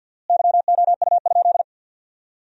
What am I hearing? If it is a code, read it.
Morse code, heard as a keyed tone, about 47 wpm, 700 Hz. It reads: YOU?
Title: 4x Word-Spacing - Sentences from Top 100 Words - 50wpm